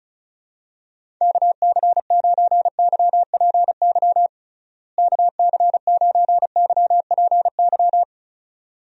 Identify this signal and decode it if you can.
Morse code, heard as a keyed tone, about 36 wpm, 700 Hz. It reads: KC9YPY KC9YPY
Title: USA Call Signs - 6 Characters - Repeated 2x - 35wpm